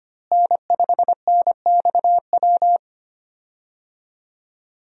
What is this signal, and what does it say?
Morse code, heard as a keyed tone, about 25 wpm, 700 Hz. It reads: N5NXW
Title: USA Call Signs - 5 Characters - Repeated 2x - 25wpm